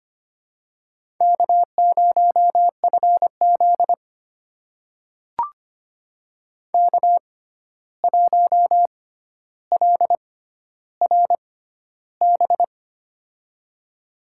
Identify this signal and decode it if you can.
Morse code, heard as a keyed tone, about 25 wpm, spaced as for 12 wpm, 700 Hz. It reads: K0FZ K1LRB
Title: USA Call Signs - 6x Character Spacing - 25wpm